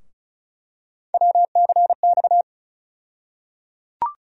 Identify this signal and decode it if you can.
Morse code, heard as a keyed tone, about 36 wpm, 700 Hz. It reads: WCX
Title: USA Call Sign Postfixes - 3 Letters 35wpm